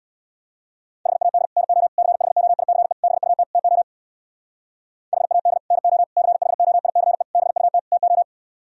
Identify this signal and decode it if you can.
Morse code, heard as a keyed tone, about 64 wpm, 690 Hz. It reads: HAD NO CHOICE BUT TO HAD NO CHOICE BUT TO